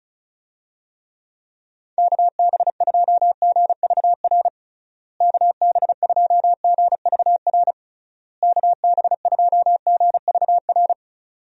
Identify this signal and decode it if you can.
Morse code, heard as a keyed tone, about 34 wpm, 700 Hz. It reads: KB2GVR KB2GVR KB2GVR
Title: USA Call Signs - 6 Characters - Repeated 3x - 35wpm